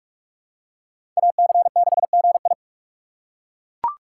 Morse code, heard as a keyed tone, about 41 wpm, 700 Hz. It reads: AC6GI
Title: USA Call Signs - 5 Characters - 6x Character Spacing - 45wpm